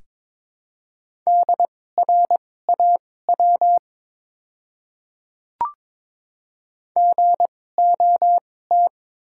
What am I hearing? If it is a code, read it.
Morse code, heard as a keyed tone, about 22 wpm, spaced as for 16 wpm, 700 Hz. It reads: DRAW GOT
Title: 2x Character Spacing - Top 300 Words - 22wpm